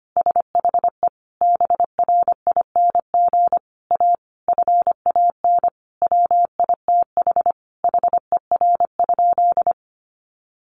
Morse code, heard as a keyed tone, about 25 wpm, 700 Hz. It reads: SHE BRING A FAN WITH HER?
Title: Sentences from Top 1300 Words 25wpm